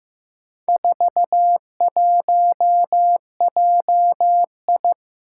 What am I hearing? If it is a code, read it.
Morse code, heard as a keyed tone, about 15 wpm, 700 Hz. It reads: 41JI
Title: Four Challenging Letter-Number Combinations 15wpm